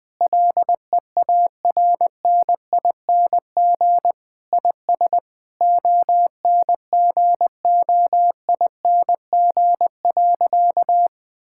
Morse code, heard as a keyed tone, about 20 wpm, 700 Hz. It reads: LEARNING IS ONGOING.
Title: Sentences with 3-Word Phrases - Spoken Beginning - 20wpm